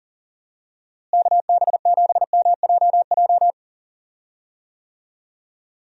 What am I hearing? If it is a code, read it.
Morse code, heard as a keyed tone, about 40 wpm, 700 Hz. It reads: KB7MJJ
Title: USA Call Signs - 6 Characters - Repeated 3x - 40wpm